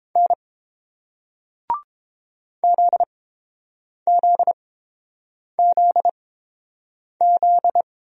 Morse code, heard as a keyed tone, about 30 wpm, spaced as for 7 wpm, 700 Hz. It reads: N ZZZZ